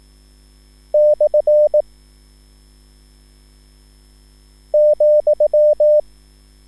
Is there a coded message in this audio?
Yes: Morse code from 0.9 to 6.0 s, heard as a keyed tone, about 18 wpm, 600 Hz. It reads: / ,